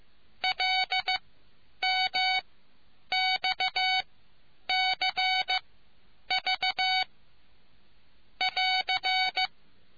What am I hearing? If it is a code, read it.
Morse code, heard as a keyed tone, about 15 wpm, spaced as for 8 wpm, 2250 Hz. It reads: LMXCV<AR>